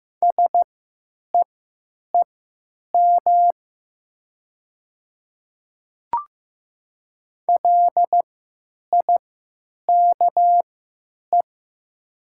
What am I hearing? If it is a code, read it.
Morse code, heard as a keyed tone, about 15 wpm, spaced as for 9 wpm, 700 Hz. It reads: SEEM LIKE
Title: 3x Character Spacing - Top 300 Words - 15wpm